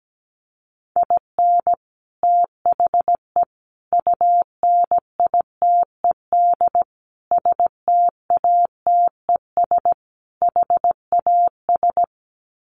Morse code, heard as a keyed tone, about 17 wpm, 700 Hz. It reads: IN THE UNITED STATES HAS
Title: Sets of 5 Words Encore2 17wpm